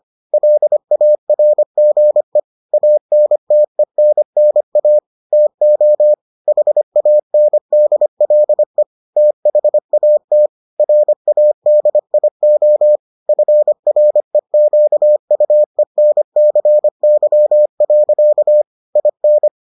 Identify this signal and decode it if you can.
Morse code, heard as a keyed tone, about 25 wpm, 600 Hz. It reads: LARGE ANTENNA TO HANDLE THAT RADIO FREQUENCY. IN